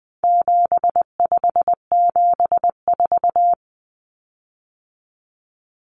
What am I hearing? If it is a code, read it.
Morse code, heard as a keyed tone, about 20 wpm, 700 Hz. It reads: 7574
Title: Common Cut Numbers - 4-Digit 20wpm